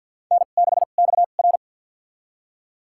Morse code, heard as a keyed tone, about 50 wpm, spaced as for 36 wpm, 700 Hz. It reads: N6XR
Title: USA Call Signs - 4 Characters - 2x Character Spacing - 50wpm